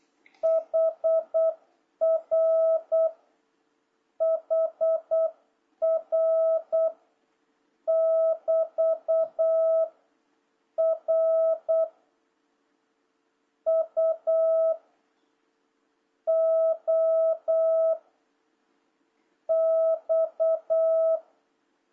Morse code, heard as a keyed tone, about 8 wpm, 650 Hz. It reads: HR HR <BT> R U O X
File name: jh8cbh / 2026cw